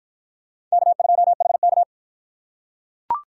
Morse code, heard as a keyed tone, about 45 wpm, 700 Hz. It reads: K2HK